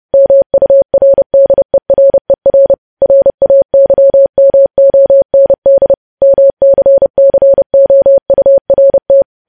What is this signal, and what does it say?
Morse code, heard as a keyed tone, about 30 wpm, 550 Hz. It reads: MURDERER RAYMOND MCCOURT
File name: podcast / News headlines-1x-30x30